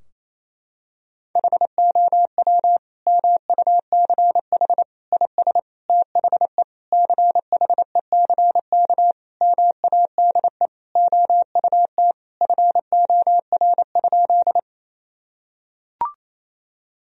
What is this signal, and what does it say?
Morse code, heard as a keyed tone, about 28 wpm, 700 Hz. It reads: HOW MUCH IS THE CHECK MADE OUT FOR?
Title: Sentences from Top 1000 Words - Review Part 3 - 28wpm